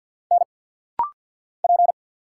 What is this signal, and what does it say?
Morse code, heard as a keyed tone, about 50 wpm, 700 Hz. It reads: N P
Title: Single Letter-Number Alt - Rapid-Fire 50wpm